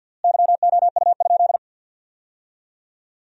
Morse code, heard as a keyed tone, about 47 wpm, 700 Hz. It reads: YOU?
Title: 4x Word-Spacing - Sentences from Top 300 Words - 50wpm